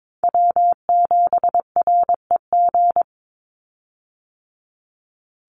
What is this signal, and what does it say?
Morse code, heard as a keyed tone, about 22 wpm, 700 Hz. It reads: W7REG
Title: USA Call Signs 22wpm